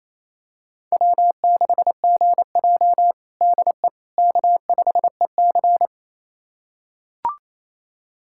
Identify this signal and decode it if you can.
Morse code, heard as a keyed tone, about 28 wpm, 700 Hz. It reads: W6GJ DE K5EC
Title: USA Callsign DE Callsign - 4 Characters - 28wpm